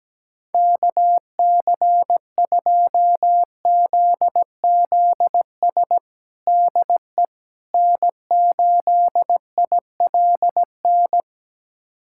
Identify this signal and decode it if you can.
Morse code, heard as a keyed tone, about 17 wpm, 700 Hz. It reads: KC2ZZS DE N8ILN
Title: USA Callsign DE Callsign 17wpm